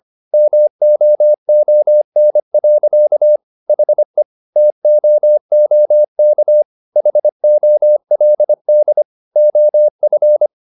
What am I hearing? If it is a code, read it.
Morse code, heard as a keyed tone, about 25 wpm, 600 Hz. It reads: MOON. HE TOOK HOLD OF
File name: the old man and the sea part 1 25wpm